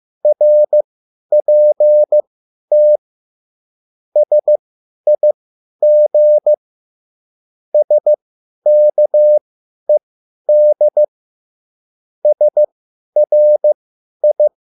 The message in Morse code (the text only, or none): RPT SIG SKED SRI